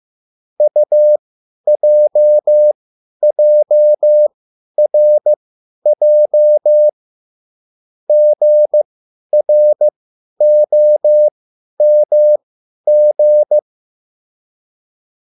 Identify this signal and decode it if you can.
Morse code, heard as a keyed tone, about 15 wpm, spaced as for 10 wpm, 600 Hz. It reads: UJJRJ GROMG